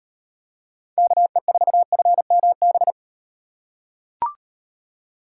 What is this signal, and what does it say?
Morse code, heard as a keyed tone, about 38 wpm, 700 Hz. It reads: KE4FMB